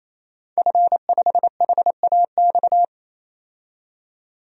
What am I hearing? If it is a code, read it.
Morse code, heard as a keyed tone, about 28 wpm, 700 Hz. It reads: F5HAX
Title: DXCC-100 Call Signs - 5 Characters - 28wpm